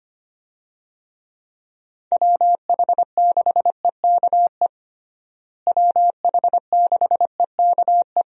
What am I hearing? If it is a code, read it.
Morse code, heard as a keyed tone, about 25 wpm, 700 Hz. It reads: WH6EKE WH6EKE
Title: USA Call Signs - 6 Characters - Repeated 2x - 25wpm